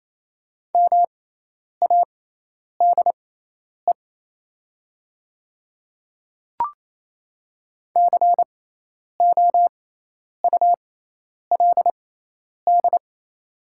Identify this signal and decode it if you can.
Morse code, heard as a keyed tone, about 27 wpm, spaced as for 10 wpm, 690 Hz. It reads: MADE COULD
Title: 6x Character Spacing - Top 300 Words - 28wpm